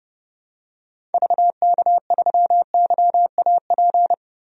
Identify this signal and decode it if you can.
Morse code, heard as a keyed tone, about 31 wpm, 700 Hz. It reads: VK3YAP
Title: DXCC-100 Call Signs - 6 Characters - 30wpm